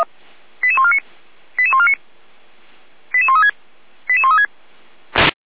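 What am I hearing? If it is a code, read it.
Five-tone selective call (ZVEI), heard as quick runs of five tones, tones of about 70 ms. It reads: 88149 88149 88136 88136
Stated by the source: ZVEI five-tone call received by SDR.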